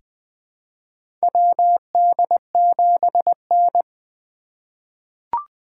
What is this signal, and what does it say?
Morse code, heard as a keyed tone, about 21 wpm, 700 Hz. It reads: WD7N
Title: USA Call Signs - 4 Characters - 3x Character Spacing - 20wpm